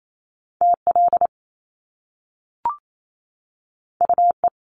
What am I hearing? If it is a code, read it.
Morse code, heard as a keyed tone, about 28 wpm, 700 Hz. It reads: TL UE
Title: Top Two Letter Combinations in Words 28wpm